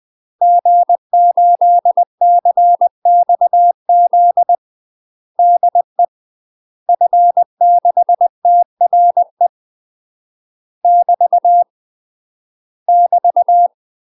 Morse code, heard as a keyed tone, about 20 wpm, 700 Hz. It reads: G8CXZ DE F6TRE <BT> <BT>
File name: UK MASTER QSOs TRAINING / 20WPM / UK MASTER QSOs 0006